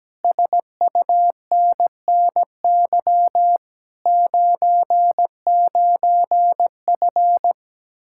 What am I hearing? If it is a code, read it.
Morse code, heard as a keyed tone, about 17 wpm, 700 Hz. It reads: SUNNY 99F
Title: Speed-Racing Common QSO Phrases Encore - 17wpm